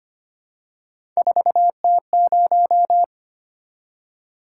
Morse code, heard as a keyed tone, about 25 wpm, 700 Hz. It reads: 4T0
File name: Three Letter-Number Combinations - 25wpm